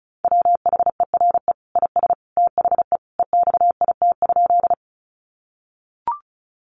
Morse code, heard as a keyed tone, about 34 wpm, 700 Hz. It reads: WHERE IS THE EXIT?
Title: Sentences from Top 1100 Words 35wpm